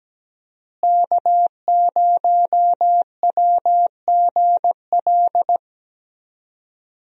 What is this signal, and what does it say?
Morse code, heard as a keyed tone, about 17 wpm, 700 Hz. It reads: K0WGL
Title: USA Call Signs - 5 Characters - Repeated 3x - 17wpm